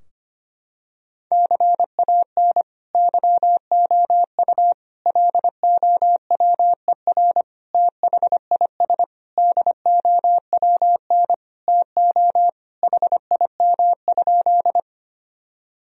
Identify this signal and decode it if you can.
Morse code, heard as a keyed tone, about 25 wpm, 700 Hz. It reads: CAN YOU LOWER THIS DOWN TO HIM?
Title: Sentences from Top 1000 Words - Review Part 4 - 25wpm